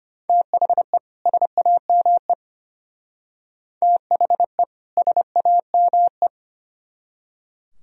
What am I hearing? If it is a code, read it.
Morse code, heard as a keyed tone, about 33 wpm, 700 Hz. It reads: THE SAME THE SAME